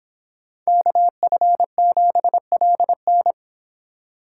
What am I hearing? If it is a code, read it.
Morse code, heard as a keyed tone, about 26 wpm, 700 Hz. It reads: KF7LN
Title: Speed Racing - USA 2x2 Call Signs 22wpm